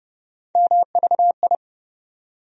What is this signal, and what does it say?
Morse code, heard as a keyed tone, about 30 wpm, 700 Hz. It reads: MVI
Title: USA Call Sign Postfixes - 3 Letters 30wpm